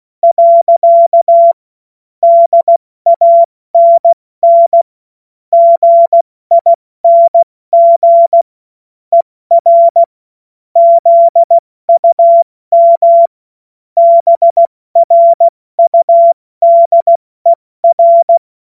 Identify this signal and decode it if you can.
Morse code, heard as a keyed tone, about 16 wpm, 680 Hz. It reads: . DANN GING ER ZUM BRUDER
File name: Morsekurs / Witze / Witz 10. August 2021, 70 F80